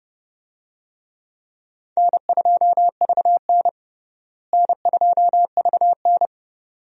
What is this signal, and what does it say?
Morse code, heard as a keyed tone, about 32 wpm, 700 Hz. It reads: N2VN N2VN
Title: USA Call Signs - 4 Characters - Repeated 2x - 30wpm